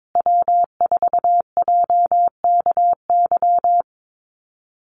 Morse code, heard as a keyed tone, about 22 wpm, 700 Hz. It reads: W4JKY
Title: USA Call Signs Repeated 2x 22wpm